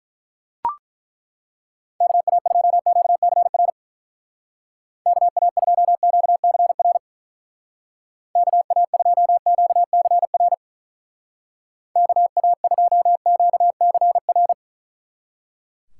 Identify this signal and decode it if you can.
Morse code, heard as a keyed tone, about 41 wpm, 700 Hz. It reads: KA2QCR KA2QCR KA2QCR KA2QCR
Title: Speed Racing - USA 2x3 Call Signs 35wpm